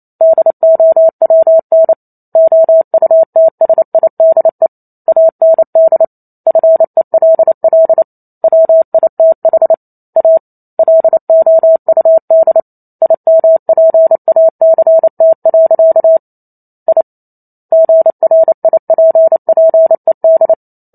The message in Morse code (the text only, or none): DOWN OUTSIDE AND FELL WITH A LOUD IMPACT. I GRIPPED